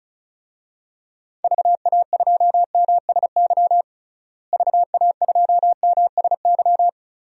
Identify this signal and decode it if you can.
Morse code, heard as a keyed tone, about 34 wpm, 700 Hz. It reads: VA2MSY VA2MSY
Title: DXCC-100 Call Signs - 6 Characters - Repeated 3x - 35wpm